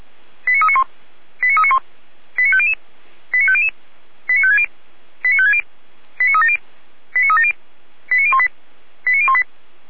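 Five-tone selective call (ZVEI), heard as quick runs of five tones, tones of about 70 ms. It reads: 89391 89391 89500 89500 89560 89560 89370 89370 89018 89018
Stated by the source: Five-tone ZVEI selective call received on an SDR.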